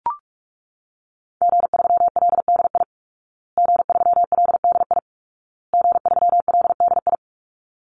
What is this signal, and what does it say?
Morse code, heard as a keyed tone, about 44 wpm, 700 Hz. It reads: G3LDI G3LDI G3LDI
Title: Call Signs Repeated 2x - 45wpm